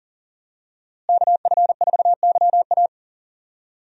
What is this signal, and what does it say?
Morse code, heard as a keyed tone, about 40 wpm, 700 Hz. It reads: KF4YA
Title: USA Call Signs - 5 Characters - Repeated 3x - 40wpm